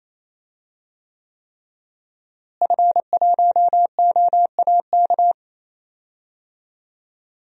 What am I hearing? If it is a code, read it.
Morse code, heard as a keyed tone, about 27 wpm, 700 Hz. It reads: F1OAK
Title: DXCC-100 Call Signs - 5 Characters - Repeated 3x - 28wpm